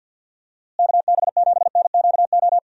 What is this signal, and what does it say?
Morse code, heard as a keyed tone, about 50 wpm, 700 Hz. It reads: KB7NQO
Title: USA Call Signs - 6 Characters - Repeated 3x - 50wpm